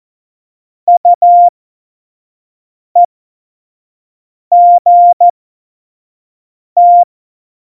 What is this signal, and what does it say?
Morse code, heard as a keyed tone, about 14 wpm, spaced as for 5 wpm, 700 Hz. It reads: UEGT
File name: Morsekurs / Witze / Witz 08. Januar 2019, 25 F70